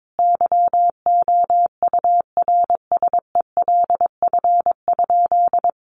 Morse code, heard as a keyed tone, about 22 wpm, 700 Hz. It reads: YOURSELF?
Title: Sentences from Top 1200 Words 22wpm